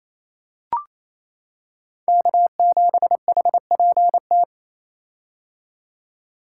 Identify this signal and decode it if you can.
Morse code, heard as a keyed tone, about 28 wpm, 700 Hz. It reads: K7HPT
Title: K1USN SST QSO Practice - 28wpm